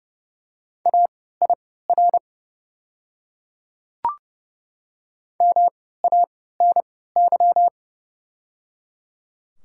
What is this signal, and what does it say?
Morse code, heard as a keyed tone, about 29 wpm, spaced as for 17 wpm, 700 Hz. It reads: AIR MANY
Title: 3x Character Spacing - Top 200 Words - 30wpm